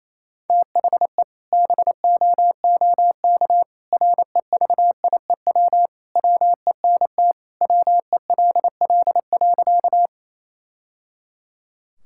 Morse code, heard as a keyed tone, about 28 wpm, 700 Hz. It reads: THE BOOK REVIEW WENT WELL.
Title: Sentences from Top 1000 Words - Review Part 6 - 28wpm